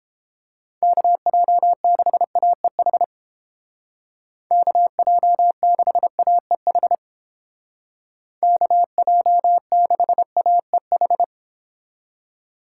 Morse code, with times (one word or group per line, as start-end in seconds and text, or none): 0.8-3.0 KJ6AEH
4.5-7.0 KJ6AEH
8.4-11.2 KJ6AEH